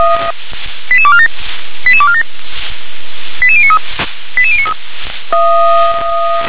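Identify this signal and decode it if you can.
Five-tone selective call (ZVEI), heard as quick runs of five tones, tones of about 70 ms. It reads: 88247 88247 88093 88093
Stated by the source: ZVEI five-tone call received by SDR.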